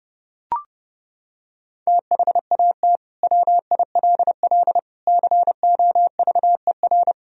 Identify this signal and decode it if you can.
Morse code, heard as a keyed tone, about 30 wpm, 700 Hz. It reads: THAT WILL COVER
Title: Sentences from Top 1000 Words - Review Part 1 - 30wpm